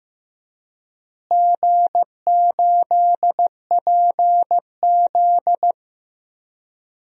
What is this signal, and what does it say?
Morse code, heard as a keyed tone, about 15 wpm, 700 Hz. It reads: G8PZ
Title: DXCC-100 Call Signs - 4 Characters - Repeated 2x - 15wpm